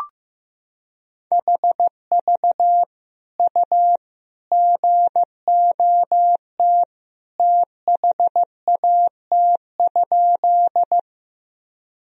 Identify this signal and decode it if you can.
Morse code, heard as a keyed tone, about 15 wpm, 700 Hz. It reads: HV U GOT THAT?
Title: Sentences from Top 300 Words and Abbreviations 15wpm